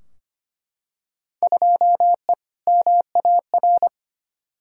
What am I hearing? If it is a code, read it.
Morse code, heard as a keyed tone, about 25 wpm, 700 Hz. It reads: 2E MAR
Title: Field Day Exchanges Encore - 25wpm